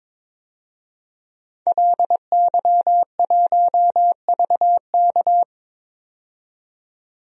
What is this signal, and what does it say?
Morse code, heard as a keyed tone, about 21 wpm, 700 Hz. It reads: LY1VK
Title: DXCC-100 Call Signs - 5 Characters - 22wpm